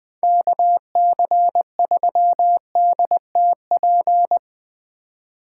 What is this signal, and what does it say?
Morse code, heard as a keyed tone, about 20 wpm, 700 Hz. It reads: KC3DTP